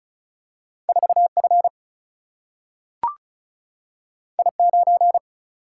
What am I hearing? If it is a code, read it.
Morse code, heard as a keyed tone, about 36 wpm, 700 Hz. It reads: 4F I9
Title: Single-Letter Single-Number 35wpm